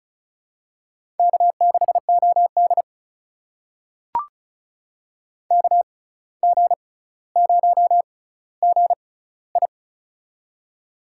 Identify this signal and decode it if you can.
Morse code, heard as a keyed tone, about 36 wpm, spaced as for 17 wpm, 700 Hz. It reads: K6OD KG0GI
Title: USA Call Signs - 6x Character Spacing - 35wpm